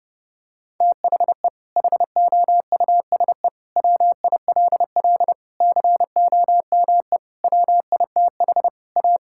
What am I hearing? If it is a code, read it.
Morse code, heard as a keyed tone, about 30 wpm, 700 Hz. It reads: THE HOUSE WILL COME WITH A
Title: Sentences from Top 1600 Words 30wpm